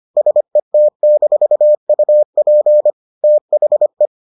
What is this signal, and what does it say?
Morse code, heard as a keyed tone, about 25 wpm, 600 Hz. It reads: SET-UP THE